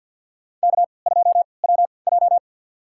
Morse code, heard as a keyed tone, about 47 wpm, spaced as for 27 wpm, 700 Hz. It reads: K2WJ
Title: USA Call Signs - 4 Characters - 3x Character Spacing - 50wpm